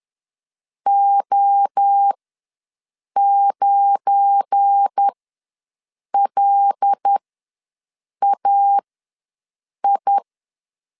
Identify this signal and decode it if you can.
Morse code, heard as a keyed tone, about 10 wpm, spaced as for 6 wpm, 800 Hz. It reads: O9LAI
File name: LEZIONI / 7 GF9Z / 7 GF9 ws 10wpm 4034